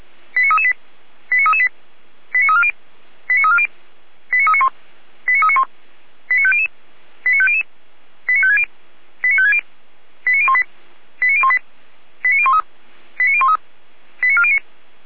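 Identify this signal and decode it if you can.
Five-tone selective call (ZVEI), heard as quick runs of five tones, tones of about 70 ms. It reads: 89338 89338 89340 89340 89391 89391 89500 89500 89560 89560 89018 89018 89013 89013 89409
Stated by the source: ZVEI five-tone call received on an SDR.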